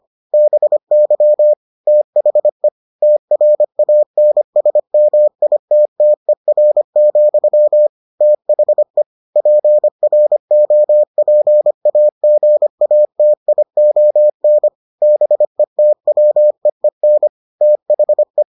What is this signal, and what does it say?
Morse code, heard as a keyed tone, about 25 wpm, 600 Hz. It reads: BY THE TRANSMITTER, THE PROPAGATION BETWEEN THE